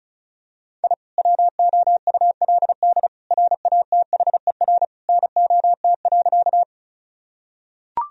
Morse code, heard as a keyed tone, about 36 wpm, 700 Hz. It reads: I WOULD RATHER NOT.
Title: Sentences from Top 1000 Words - Review Part 4 - 35wpm